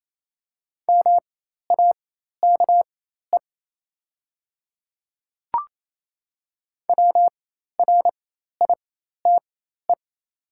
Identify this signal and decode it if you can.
Morse code, heard as a keyed tone, about 27 wpm, spaced as for 13 wpm, 700 Hz. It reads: MAKE WRITE
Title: 4x Character Spacing - Top 100 Words - 28wpm